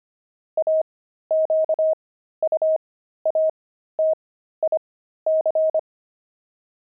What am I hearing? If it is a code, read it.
Morse code, heard as a keyed tone, about 25 wpm, spaced as for 13 wpm, 630 Hz. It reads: AQUATIC
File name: pangram-fs-13-cs-25-journey